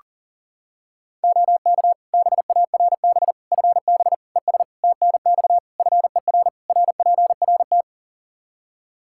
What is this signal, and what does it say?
Morse code, heard as a keyed tone, about 37 wpm, 700 Hz. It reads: OK BARB FB ES TNX FER RPRT
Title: Practice QSO - Exchange 2 - Info - 40wpm